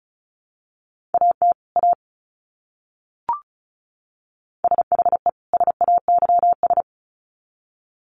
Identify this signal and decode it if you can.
Morse code, heard as a keyed tone, about 34 wpm, 700 Hz. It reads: AT A SHE SAYS
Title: Sets of 2 Words 35wpm